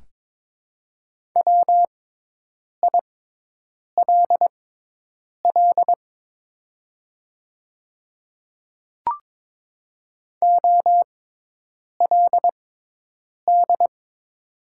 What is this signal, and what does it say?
Morse code, heard as a keyed tone, about 22 wpm, spaced as for 8 wpm, 700 Hz. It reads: WILL OLD
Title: 6x Character Spacing - Top 200 Words - 22wpm